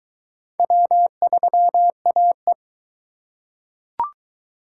Morse code, heard as a keyed tone, about 19 wpm, 700 Hz. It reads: W3AE